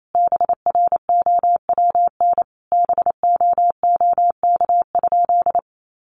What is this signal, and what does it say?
Morse code, heard as a keyed tone, about 28 wpm, 700 Hz. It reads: BROWN BOOK?
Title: Sentences from Top 1200 Words 28wpm